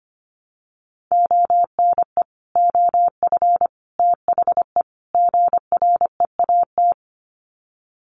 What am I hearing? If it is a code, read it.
Morse code, heard as a keyed tone, about 25 wpm, 700 Hz. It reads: ONE OF THE GREAT
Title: Sets of 4 Words 25wpm